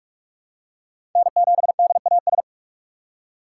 Morse code, heard as a keyed tone, about 46 wpm, 700 Hz. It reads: N7DAS
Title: USA Call Signs - 6x Character Spacing - 45wpm